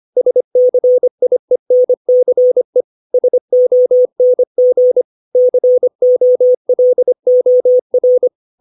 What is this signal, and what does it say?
Morse code, heard as a keyed tone, about 25 wpm, 500 Hz. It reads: SCIENCE SONG COLOR
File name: 632 words / 632 Words 25WPM